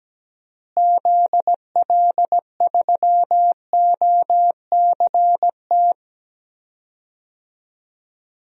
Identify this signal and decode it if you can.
Morse code, heard as a keyed tone, about 17 wpm, 700 Hz. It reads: ZL3OCT
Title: DXCC-100 Call Signs - 6 Characters - Repeated 3x - 17wpm